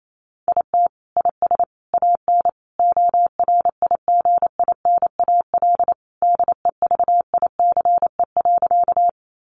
Morse code, heard as a keyed tone, about 28 wpm, 700 Hz. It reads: IT IS AN ORIGINAL DEVICE.
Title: Sentences from Top 1500 Words 28wpm